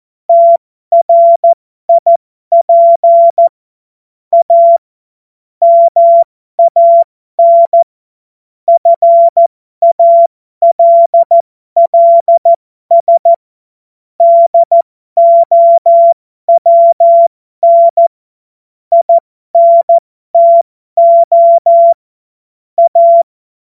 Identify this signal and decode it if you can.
Morse code, heard as a keyed tone, about 14 wpm, spaced as for 12 wpm, 680 Hz. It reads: TRIP A MAN FALLS DOWN INTO A
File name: Morsekurs / Witze